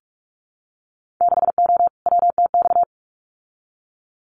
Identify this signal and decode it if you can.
Morse code, heard as a keyed tone, about 44 wpm, 690 Hz. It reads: 6O WTX
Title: Winter Field Day Exchanges Encore 45wpm